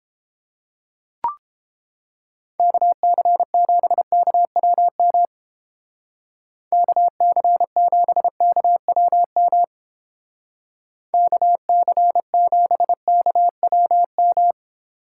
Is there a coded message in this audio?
Yes: Morse code from 2.6 to 14.5 s, heard as a keyed tone, about 29 wpm, 700 Hz. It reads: KC7KWM KC7KWM KC7KWM